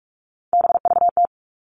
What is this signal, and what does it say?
Morse code, heard as a keyed tone, about 44 wpm, 700 Hz. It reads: BVT
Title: All Cut Numbers - 3-Digit 45wpm